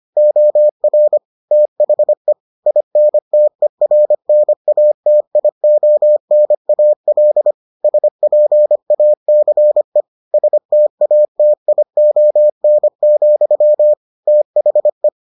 Morse code, heard as a keyed tone, about 25 wpm, 600 Hz. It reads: OR THE INTERNATIONAL SPACE STATION, THE